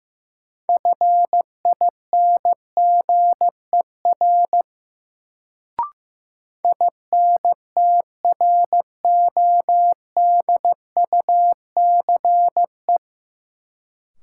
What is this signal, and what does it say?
Morse code, heard as a keyed tone, about 15 wpm, 700 Hz. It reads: FINGER INTRODUCE
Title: Sentences from Top 1600 Words - New Word Drill Alt - 15wpm